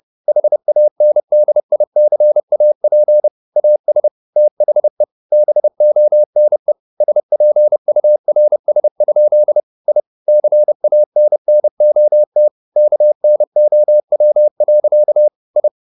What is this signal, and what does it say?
Morse code, heard as a keyed tone, about 30 wpm, 600 Hz. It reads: HANDICAP AS THE BONE SPURS? I CANNOT KNOW. I